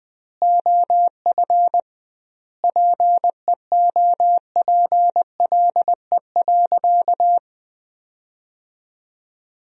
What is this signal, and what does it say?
Morse code, heard as a keyed tone, about 20 wpm, 700 Hz. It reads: OF PEOPLE.
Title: 2x Word-Spacing - Sentences from Top 500 Words - 20wpm